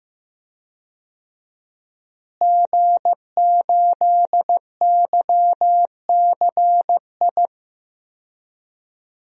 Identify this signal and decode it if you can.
Morse code, heard as a keyed tone, about 15 wpm, 700 Hz. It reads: G8YCI
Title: DXCC-100 Call Signs - 5 Characters - Repeated 2x - 15wpm